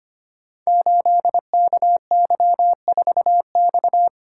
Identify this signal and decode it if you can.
Morse code, heard as a keyed tone, about 25 wpm, 700 Hz. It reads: 8KY4X
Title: Code Groups - 5 Letter:Numbers - 25wpm